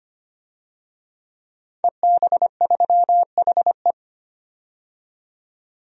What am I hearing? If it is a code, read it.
Morse code, heard as a keyed tone, about 25 wpm, 700 Hz. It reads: EB3HE
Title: DXCC-100 Call Signs - 5 Characters - 25wpm